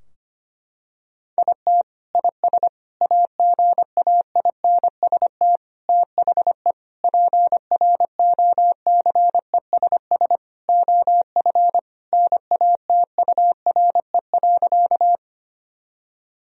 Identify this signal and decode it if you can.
Morse code, heard as a keyed tone, about 25 wpm, 700 Hz. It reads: IT IS AGAINST THE PROCESS OF NATURE.
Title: Sentences from Top 1000 Words - Review Part 3 - 25wpm